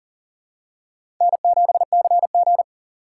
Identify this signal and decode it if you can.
Morse code, heard as a keyed tone, about 40 wpm, 700 Hz. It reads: N7CG
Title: USA Call Signs - 4 Characters - 6x Character Spacing - 40wpm